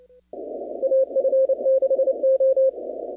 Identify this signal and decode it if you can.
Morse code, heard as a keyed tone, about 29 wpm, 530 Hz. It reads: AF6O
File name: contests / 2019-cw-wpx-kt5j-k5pi- -2330 0-41